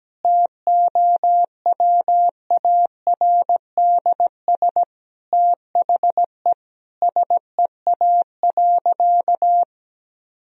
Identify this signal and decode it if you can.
Morse code, heard as a keyed tone, about 17 wpm, 700 Hz. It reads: TOWARDS THE SEA.
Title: Sentences from Top 1000 Words - Review Part 2 - 17wpm